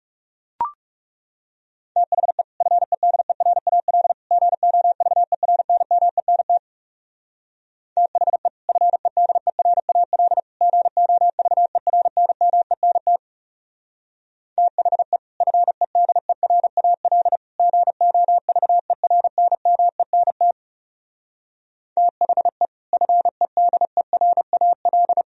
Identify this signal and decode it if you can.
Morse code, heard as a keyed tone, about 37 wpm, 700 Hz. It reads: THE FEDERAL GOVERNMENT THE FEDERAL GOVERNMENT THE FEDERAL GOVERNMENT THE FEDERAL
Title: Speed-Racing - Sets of 3 Words 30wpm